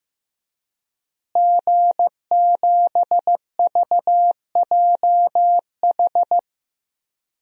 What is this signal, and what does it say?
Morse code, heard as a keyed tone, about 15 wpm, 700 Hz. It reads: G7VJH
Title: DXCC-100 Call Signs - 5 Characters - Repeated 3x - 15wpm